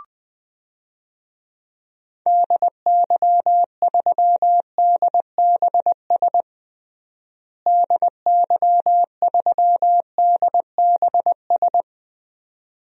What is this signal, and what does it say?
Morse code, heard as a keyed tone, about 20 wpm, 700 Hz. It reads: DY3DBS DY3DBS
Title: DXCC-100 Call Signs - 6 Characters - Repeated 2x - 20wpm